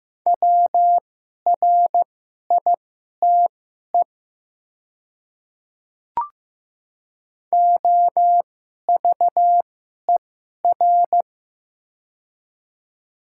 Morse code, heard as a keyed tone, about 15 wpm, spaced as for 11 wpm, 700 Hz. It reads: WRITE OVER
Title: 2x Character Spacing - Top 300 Words - 15wpm